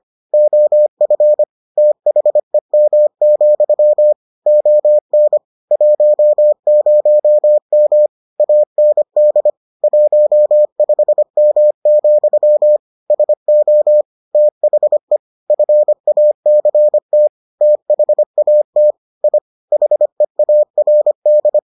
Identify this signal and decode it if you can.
Morse code, heard as a keyed tone, about 25 wpm, 600 Hz. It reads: OF THEM, ON 10M AND 15M, SO THE FACT THAT I HEARD